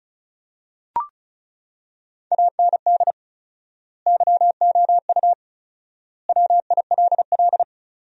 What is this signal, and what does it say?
Morse code, heard as a keyed tone, about 36 wpm, 700 Hz. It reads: AND YOU WILL
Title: 4x Word-Spacing - Sentences from Top 400 Words - 35wpm